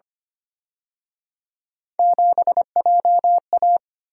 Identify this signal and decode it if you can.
Morse code, heard as a keyed tone, about 29 wpm, 700 Hz. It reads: 7JA
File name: Speed-Racing - Three Letter-Number Combinations - 25wpm